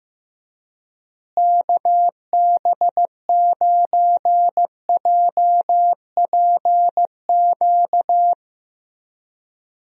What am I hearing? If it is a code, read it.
Morse code, heard as a keyed tone, about 15 wpm, 700 Hz. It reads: KB9JPQ
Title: USA Call Signs - 6 Characters - Repeated 3x - 15wpm